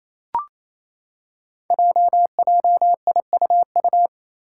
Morse code, heard as a keyed tone, about 28 wpm, 700 Hz. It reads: JJIUU
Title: Five Challenging Letter-Number Combinations 28wpm